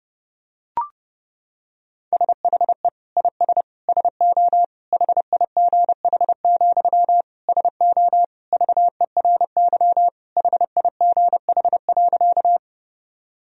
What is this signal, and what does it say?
Morse code, heard as a keyed tone, about 30 wpm, 700 Hz. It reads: SHE IS SO HIGH, SO VERY HIGH.
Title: Sentences from Top 1000 Words - Review Part 1 - 30wpm